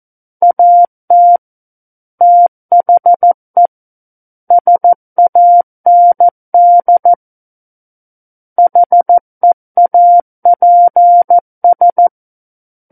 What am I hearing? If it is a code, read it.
Morse code, heard as a keyed tone, about 14 wpm, 700 Hz. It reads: AT THE SAND HEAPS
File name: Book One / 05-Chapter 5 - 14 WPM